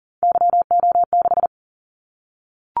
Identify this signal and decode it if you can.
Morse code, heard as a keyed tone, about 38 wpm, 700 Hz. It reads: YO6
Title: Callsign Prefixes for Top 25 DXCC - 40wpm